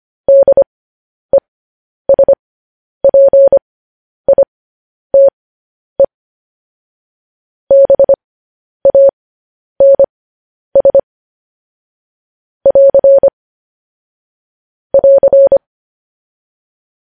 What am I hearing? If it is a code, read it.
Morse code, heard as a keyed tone, about 25 wpm, spaced as for 10 wpm, 550 Hz. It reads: DESPITE BANS <AR> <AR>
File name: podcast / News headlines-1x-25x10